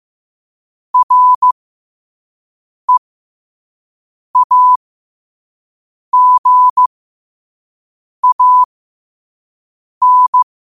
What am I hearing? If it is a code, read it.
Morse code, heard as a keyed tone, about 15 wpm, spaced as for 5 wpm, 1000 Hz. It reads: REAGAN
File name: qotdfiles / 5wpm